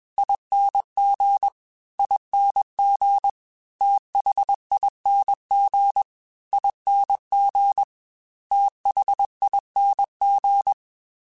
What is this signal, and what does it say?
Morse code, heard as a keyed tone, about 21 wpm, 800 Hz. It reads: ING ING THING ING THING
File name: Special INGs / Special INGs-mp3s / Slow ing @20